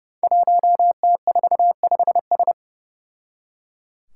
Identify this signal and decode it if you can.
Morse code, heard as a keyed tone, about 30 wpm, 700 Hz. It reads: 1T45S
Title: Five Challenging Letter-Number Combinations 30wpm